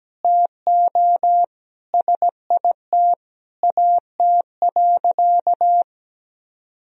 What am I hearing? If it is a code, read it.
Morse code, heard as a keyed tone, about 17 wpm, 700 Hz. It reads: TO SIT AT.
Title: Sentences from Top 1000 Words - Review Part 2 - 17wpm